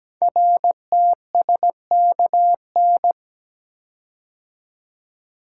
Morse code, heard as a keyed tone, about 17 wpm, 700 Hz. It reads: RTSKN
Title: Code Groups - 5 Letters - 17wpm